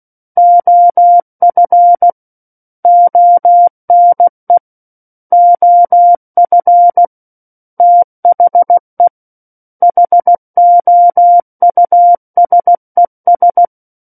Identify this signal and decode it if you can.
Morse code, heard as a keyed tone, about 16 wpm, 700 Hz. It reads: OF ONE OF THE HOUSES
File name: Book One / 07-Chapter 7 - 16 WPM